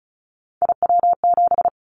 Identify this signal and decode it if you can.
Morse code, heard as a keyed tone, about 36 wpm, 700 Hz. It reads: IW7
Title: Callsign Prefixes for Top 25 DXCC - 35wpm